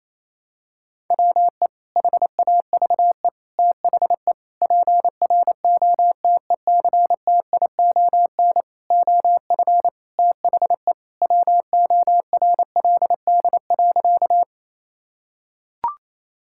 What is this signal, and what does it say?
Morse code, heard as a keyed tone, about 28 wpm, 700 Hz. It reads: WE HAVE THE PROTECTION OF THE WORLD.
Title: Sentences from Top 1600 Words 28wpm